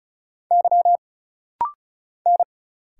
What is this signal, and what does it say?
Morse code, heard as a keyed tone, about 35 wpm, 700 Hz. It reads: Y N
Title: Single Letter-Number Alt - Rapid-Fire 35wpm